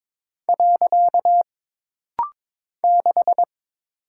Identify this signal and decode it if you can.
Morse code, heard as a keyed tone, about 21 wpm, 700 Hz. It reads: . 6
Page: This rapid-fire timing is geared to push your ICR (Instant Character Recognition) skills.